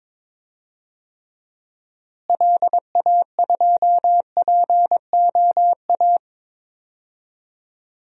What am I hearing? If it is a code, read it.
Morse code, heard as a keyed tone, about 22 wpm, 700 Hz. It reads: LA2POA
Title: DXCC-100 Call Signs - 6 Characters - Repeated 2x - 22wpm